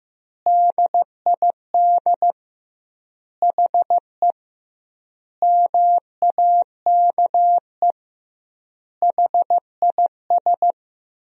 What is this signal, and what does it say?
Morse code, heard as a keyed tone, about 16 wpm, 700 Hz. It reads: DID HE MAKE HIS
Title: 2x Word-Spacing - Sentences from Top 400 Words - 15wpm